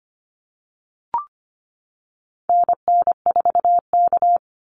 Morse code, heard as a keyed tone, about 25 wpm, 700 Hz. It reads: NN4K
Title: Call Signs - 25wpm